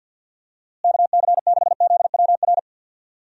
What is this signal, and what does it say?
Morse code, heard as a keyed tone, about 50 wpm, 700 Hz. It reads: KC6ZWR
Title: USA Call Signs - 6 Characters - Repeated 3x - 50wpm